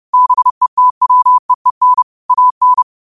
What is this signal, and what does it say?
Morse code, heard as a keyed tone, about 30 wpm, 1000 Hz. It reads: BETWEEN AN